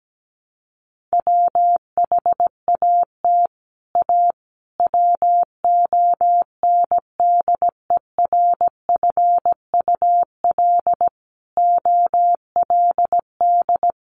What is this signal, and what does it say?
Morse code, heard as a keyed tone, about 17 wpm, 700 Hz. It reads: WHAT A WONDERFUL OLD
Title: Sentences from Top 1100 Words 17wpm